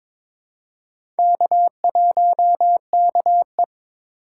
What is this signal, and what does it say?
Morse code, heard as a keyed tone, about 22 wpm, 700 Hz. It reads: K1KE
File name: practice / 22wpm / Four Letter-Number Combinations - 22wpm